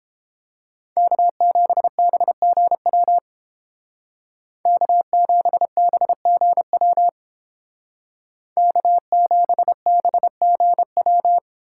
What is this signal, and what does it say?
Morse code, heard as a keyed tone, about 29 wpm, 700 Hz. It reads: K7BGW K7BGW K7BGW